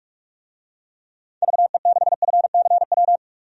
Morse code, heard as a keyed tone, about 44 wpm, 700 Hz. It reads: VE6FCW
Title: DXCC-100 Call Signs - 6 Characters - 45wpm